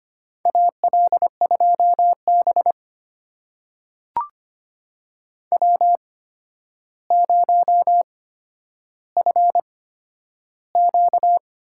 Morse code, heard as a keyed tone, about 25 wpm, spaced as for 11 wpm, 700 Hz. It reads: AL2B W0FQ
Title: USA Call Signs - 8x Character Spacing - 25wpm